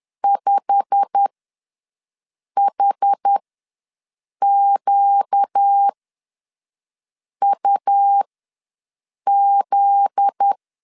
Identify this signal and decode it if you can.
Morse code, heard as a keyed tone, about 10 wpm, spaced as for 5 wpm, 800 Hz. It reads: 5HQUZ
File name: LEZIONI / 8 YJ45 / 8 YJ45 ws 10wpm 3843